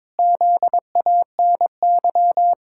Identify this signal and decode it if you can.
Morse code, heard as a keyed tone, about 22 wpm, 700 Hz. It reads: ZANY
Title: Pangram Challenge v2 - 22wpm